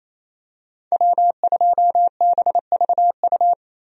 Morse code, heard as a keyed tone, about 28 wpm, 700 Hz. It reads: W2BVU
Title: USA Call Signs - 5 Characters - Repeated 2x - 28wpm